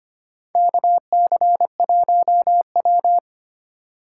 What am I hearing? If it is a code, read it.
Morse code, heard as a keyed tone, about 25 wpm, 700 Hz. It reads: KC1W
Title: USA Call Signs - 2x Character Spacing - 25wpm